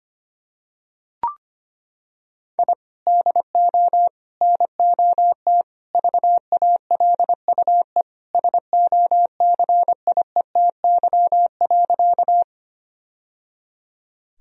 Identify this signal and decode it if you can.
Morse code, heard as a keyed tone, about 25 wpm, 700 Hz. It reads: I DO NOT VALUE SOCIETY.
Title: Sentences from Top 1000 Words - Review Part 4 - 25wpm